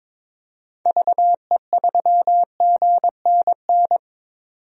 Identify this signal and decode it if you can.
Morse code, heard as a keyed tone, about 22 wpm, 700 Hz. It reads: VE3GNN
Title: DXCC-100 Call Signs - 6 Characters - 22wpm